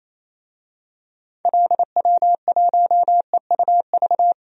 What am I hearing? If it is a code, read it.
Morse code, heard as a keyed tone, about 28 wpm, 700 Hz. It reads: LW1EUV
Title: DXCC-100 Call Signs - 6 Characters - 28wpm